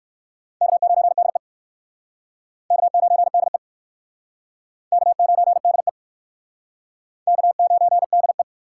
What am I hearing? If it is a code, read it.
Morse code, heard as a keyed tone, about 59 wpm, 700 Hz. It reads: K9DE K9DE K9DE K9DE